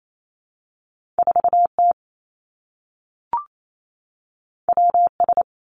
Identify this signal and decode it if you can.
Morse code, heard as a keyed tone, about 28 wpm, 700 Hz. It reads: 4 T W S
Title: Two Challenging Letter-Number Combinations 28wpm